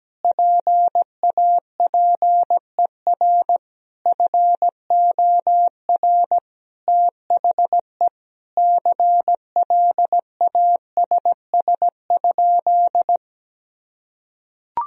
Morse code, heard as a keyed tone, about 17 wpm, 700 Hz. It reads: PAPER FOR THE CLASS?
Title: Sentences from Top 1000 Words - Review Part 2 - 17wpm